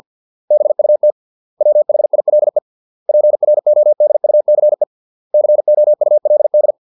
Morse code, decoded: BUT WHILE PRODUCE COULD

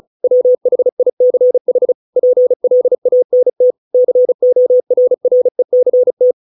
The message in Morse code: WHICH PLANT CORRECT

